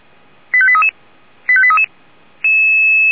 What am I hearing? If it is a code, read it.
Five-tone selective call (ZVEI), heard as quick runs of five tones, tones of about 70 ms. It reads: 85833 85833
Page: /464/SDR/ZVEI/89494/